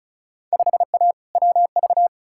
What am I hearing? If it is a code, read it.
Morse code, heard as a keyed tone, about 36 wpm, 700 Hz. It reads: 5A WV
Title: Field Day Exchanges - 35wpm